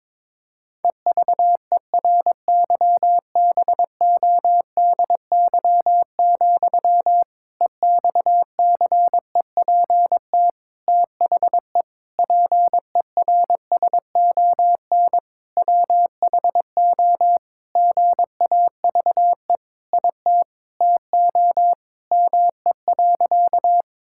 Morse code, heard as a keyed tone, about 22 wpm, 700 Hz. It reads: EVERYBODY, EXCEPT THE PERSON WHO GAVE IT TO ME.
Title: Sentences from Top 1600 Words 22wpm